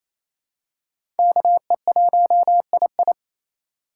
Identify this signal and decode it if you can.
Morse code, heard as a keyed tone, about 28 wpm, 700 Hz. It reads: KE1II
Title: USA Call Signs - 6x Character Spacing - 28wpm